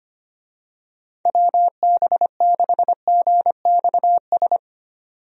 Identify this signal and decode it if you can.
Morse code, heard as a keyed tone, about 25 wpm, 700 Hz. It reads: WB6GXS
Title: USA Call Signs - 6 Characters - 25wpm